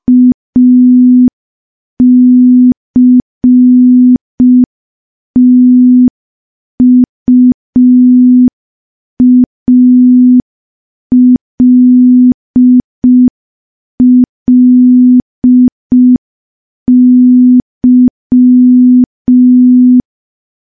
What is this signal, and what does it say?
Morse code, heard as a keyed tone, about 5 wpm, 250 Hz. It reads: ACTUALLY
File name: morse / morse-training-05WPM